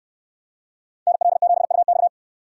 Morse code, heard as a keyed tone, about 69 wpm, 700 Hz. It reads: NL7UX